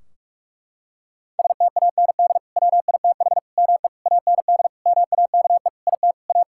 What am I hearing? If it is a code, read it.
Morse code, heard as a keyed tone, about 46 wpm, 700 Hz. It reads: STAND WITH ME AND MAKE IT A
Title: Sentences from Top 1600 Words 45wpm